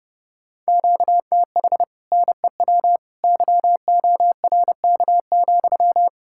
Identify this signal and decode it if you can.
Morse code, heard as a keyed tone, about 29 wpm, 700 Hz. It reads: QTH NEW YORK,